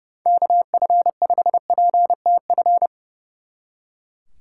Morse code, heard as a keyed tone, about 39 wpm, 700 Hz. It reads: KF5PTF